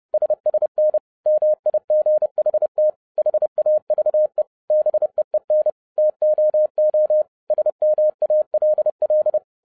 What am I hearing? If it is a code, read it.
Morse code, heard as a keyed tone, about 30 wpm, 610 Hz. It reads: SSN MIGHT HAVE BEEN TOO SMALL
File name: 2012-01-mp3 / n11-I DIDNT MISS THE SOLAR CYCLE 24-30wpm